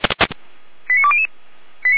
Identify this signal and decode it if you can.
Five-tone selective call (ZVEI), heard as quick runs of five tones, tones of about 70 ms. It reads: 89200
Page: audio.f85360.feuerwehr-orlamuende.org